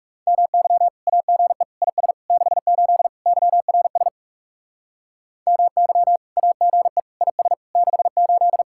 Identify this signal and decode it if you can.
Morse code, heard as a keyed tone, about 35 wpm, 700 Hz. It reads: MY AGE IS 68 YRS MY AGE IS 68
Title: Speed-Racing Common QSO Phrases Encore - 30wpm